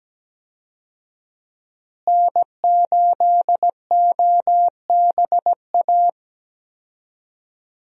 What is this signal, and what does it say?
Morse code, heard as a keyed tone, about 17 wpm, 700 Hz. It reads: N8OBA
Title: USA Call Signs - 5 Characters - Repeated 3x - 17wpm